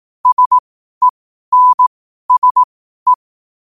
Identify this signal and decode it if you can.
Morse code, heard as a keyed tone, about 18 wpm, spaced as for 12 wpm, 1000 Hz. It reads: SENSE
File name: qotdfiles / 12wpm